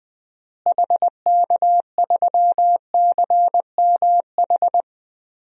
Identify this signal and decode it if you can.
Morse code, heard as a keyed tone, about 20 wpm, 700 Hz. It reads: HK3CMH
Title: DXCC-100 Call Signs - 6 Characters - Repeated 3x - 20wpm